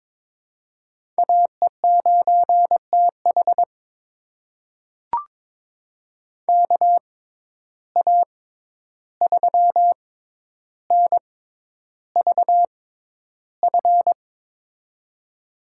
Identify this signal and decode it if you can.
Morse code, heard as a keyed tone, about 22 wpm, spaced as for 11 wpm, 700 Hz. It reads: AE9TH KA3NVF
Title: USA Call Signs - 6x Character Spacing - 22wpm